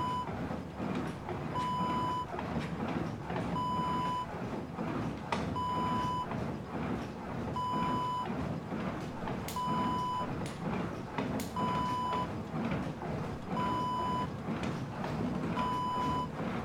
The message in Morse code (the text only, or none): none